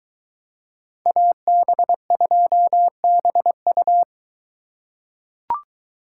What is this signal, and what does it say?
Morse code, heard as a keyed tone, about 22 wpm, 700 Hz. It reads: AB2BU